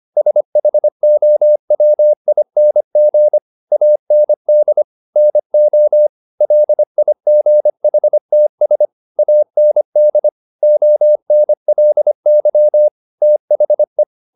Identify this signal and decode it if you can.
Morse code, heard as a keyed tone, about 25 wpm, 600 Hz. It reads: SHOWING AND NO LIGHTS AND ONLY THE